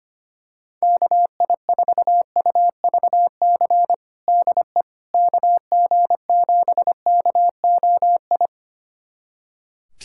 Morse code, heard as a keyed tone, about 25 wpm, 700 Hz. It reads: KI4UVC DE KG7KOI
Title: USA Callsign DE Callsign - 6 Characters - 25wpm